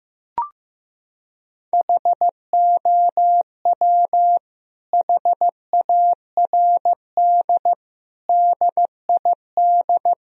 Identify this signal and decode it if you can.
Morse code, heard as a keyed tone, about 15 wpm, 700 Hz. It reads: HOW HARD DID
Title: Sentences from Top 1000 Words - Review Part 4 - 15wpm